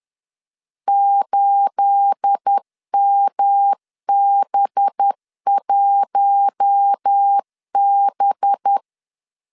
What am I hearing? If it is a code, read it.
Morse code, heard as a keyed tone, about 10 wpm, 800 Hz. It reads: 8MB1B
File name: CW / DET / 1 9 1500C S A 10wpm 4511